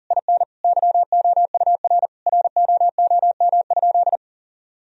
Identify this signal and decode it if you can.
Morse code, heard as a keyed tone, about 37 wpm, 700 Hz. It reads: IN YOUR ROOM?
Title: Sentences from Top 1000 Words - Review Part 4 - 40wpm